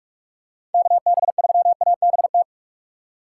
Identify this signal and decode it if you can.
Morse code, heard as a keyed tone, about 46 wpm, 700 Hz. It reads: KB3ABT
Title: USA Call Signs - 6 Characters - 4x Character Spacing - 45wpm